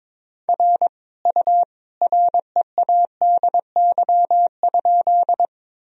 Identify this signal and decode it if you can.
Morse code, heard as a keyed tone, about 22 wpm, 700 Hz. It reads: R U READY?
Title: Sentences from Top 400 Words and Abbreviations 22wpm